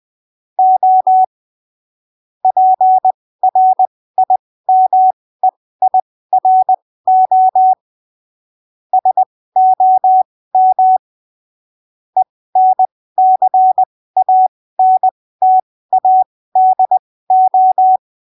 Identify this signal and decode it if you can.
Morse code, heard as a keyed tone, about 20 wpm, spaced as for 15 wpm, 750 Hz. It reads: O PRIMEIRO SOM ENCANTADO
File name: arquivos / texto / texto 29 20ppm